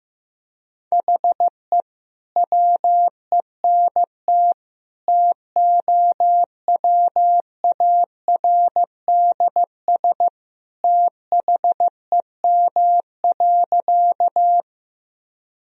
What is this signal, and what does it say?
Morse code, heard as a keyed tone, about 16 wpm, 700 Hz. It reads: HE WENT TOWARDS THEM.
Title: Sentences from Top 1000 Words - Review Part 2 - 15wpm